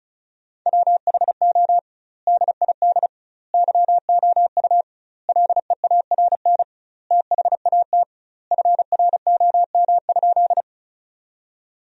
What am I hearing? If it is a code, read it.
Morse code, heard as a keyed tone, about 34 wpm, 700 Hz. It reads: WHO DID YOU LEARN THAT FROM?
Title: 2x Word-Spacing - Sentences from Top 300 Words - 35wpm